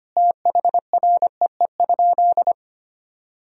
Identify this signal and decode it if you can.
Morse code, heard as a keyed tone, about 25 wpm, 700 Hz. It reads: THREE?
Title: Sentences from Top 1000 Words - Review Part 2 - 25wpm